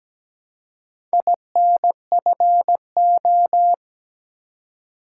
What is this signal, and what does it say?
Morse code, heard as a keyed tone, about 17 wpm, 700 Hz. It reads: INFO
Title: Top 180 QSO Elements 17wpm